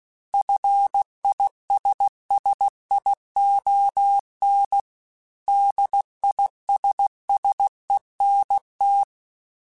Morse code, heard as a keyed tone, about 16 wpm, 800 Hz. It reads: FISSION DISSENT
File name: General / General-mp3s / ISS @15